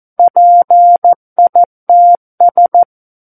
Morse code, heard as a keyed tone, about 14 wpm, 700 Hz. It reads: PITS
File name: Book One / 05-Chapter 5 - 14 WPM